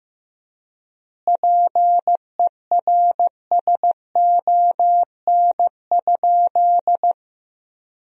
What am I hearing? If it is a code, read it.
Morse code, heard as a keyed tone, about 15 wpm, 700 Hz. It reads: PERSON?
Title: 4x Word-Spacing - Sentences from Top 400 Words - 15wpm